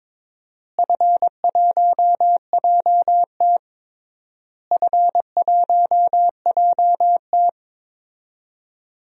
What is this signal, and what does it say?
Morse code, heard as a keyed tone, about 22 wpm, 700 Hz. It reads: F1JT F1JT
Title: DXCC-100 Call Signs - 4 Characters - Repeated 3x - 22wpm